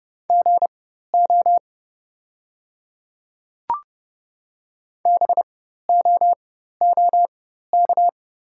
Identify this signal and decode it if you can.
Morse code, heard as a keyed tone, about 30 wpm, spaced as for 14 wpm, 700 Hz. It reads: GO BOOK
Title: 4x Character Spacing - Top 300 Words - 30wpm